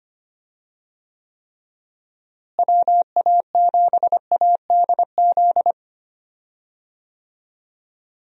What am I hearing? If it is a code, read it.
Morse code, heard as a keyed tone, about 25 wpm, 700 Hz. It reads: WA7ADZ
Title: USA Call Signs - 6 Characters - Repeated 2x - 25wpm